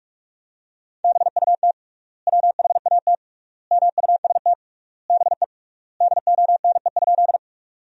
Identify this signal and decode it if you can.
Morse code, heard as a keyed tone, about 46 wpm, 700 Hz. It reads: BUT WHAT MUST BE DONE?
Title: 3x Word-Spacing - Sentences from Top 500 Words - 45wpm